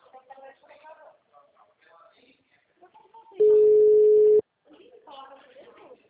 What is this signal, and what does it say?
Ringback tone (Europe, 425 Hz), with speech.
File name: out- -2011- - - .37931